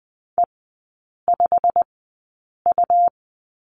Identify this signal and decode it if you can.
Morse code, heard as a keyed tone, about 20 wpm, spaced as for 8 wpm, 700 Hz. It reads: E5U
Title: 100 Lesson 17e - Letter u - Three Characters 20wpm